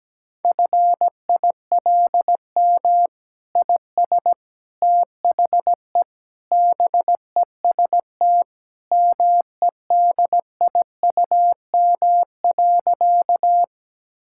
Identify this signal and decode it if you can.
Morse code, heard as a keyed tone, about 17 wpm, 700 Hz. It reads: FILM IS THE BEST MEDIUM.
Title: Sentences from Top 1000 Words - Review Part 6 - 17wpm